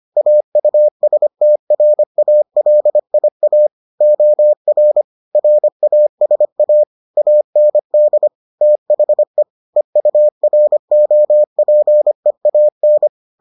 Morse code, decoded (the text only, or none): AUSTRALIA OR RASA AND THE EUROPEAN